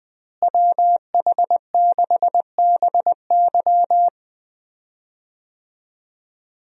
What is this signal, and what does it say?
Morse code, heard as a keyed tone, about 20 wpm, 700 Hz. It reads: WH6BY
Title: USA Call Signs - 5 Characters - Repeated 2x - 20wpm